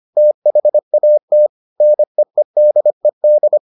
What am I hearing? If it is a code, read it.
Morse code, heard as a keyed tone, about 25 wpm, 600 Hz. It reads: THAT NEEDED